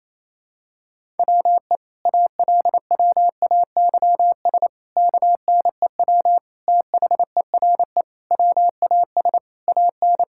Morse code, heard as a keyed tone, about 28 wpm, 700 Hz. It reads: WE ALWAYS KNEW THERE WAS AN
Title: Sentences from Top 1000 Words - Review Part 6 - 28wpm